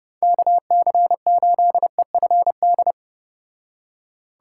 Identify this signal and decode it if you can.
Morse code, heard as a keyed tone, about 31 wpm, 700 Hz. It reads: KC8EFD